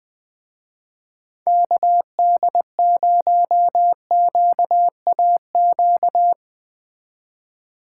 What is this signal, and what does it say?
Morse code, heard as a keyed tone, about 20 wpm, 700 Hz. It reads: KD0QAQ